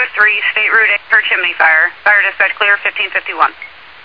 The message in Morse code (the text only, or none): none